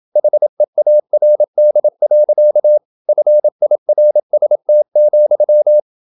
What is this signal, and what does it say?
Morse code, heard as a keyed tone, about 28 wpm, 600 Hz. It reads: HEARD. FIRST,
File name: Never 0027wpm